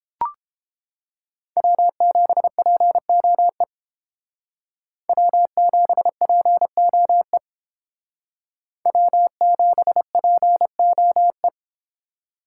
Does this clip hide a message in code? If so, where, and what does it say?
Yes: Morse code from 1.6 to 11.5 s, heard as a keyed tone, about 30 wpm, 700 Hz. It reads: W7POE W7POE W7POE